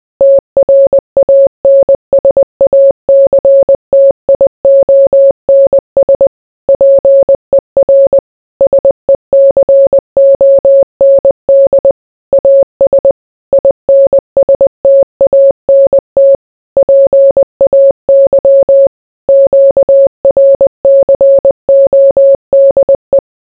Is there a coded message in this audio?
Yes: Morse code from 0.2 to 23.2 s, heard as a keyed tone, about 20 wpm, 550 Hz. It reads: TRANSACTIONS PER SECOND AS INSTANT PAY QRCODE